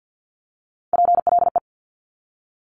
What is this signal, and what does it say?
Morse code, heard as a keyed tone, about 50 wpm, 700 Hz. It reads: PLE
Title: Top Three Letter Combinations in Words 50wpm